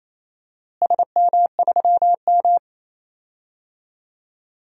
Morse code, heard as a keyed tone, about 27 wpm, 700 Hz. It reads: SM3M
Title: DXCC-100 Call Signs - 4 Characters - Repeated 3x - 28wpm